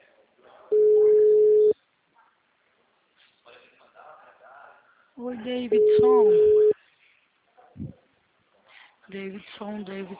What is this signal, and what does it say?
Ringback tone (Europe, 425 Hz), with speech.